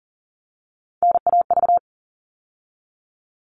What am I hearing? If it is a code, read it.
Morse code, heard as a keyed tone, about 40 wpm, 700 Hz. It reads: NAV
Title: All Cut Numbers - 3-Digit 40wpm